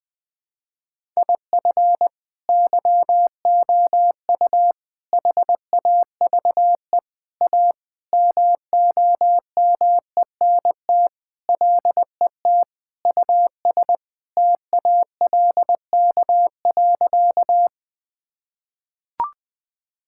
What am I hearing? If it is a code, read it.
Morse code, heard as a keyed tone, about 21 wpm, 700 Hz. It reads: IF YOU HAVE A MOMENT LET US TALK.
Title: Sentences with 4-Word Phrases - 20wpm